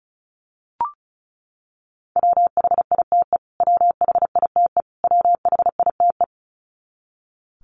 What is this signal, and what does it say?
Morse code, heard as a keyed tone, about 34 wpm, 700 Hz. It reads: WHITE WHITE WHITE
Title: Top 500 Words Repeated 35wpm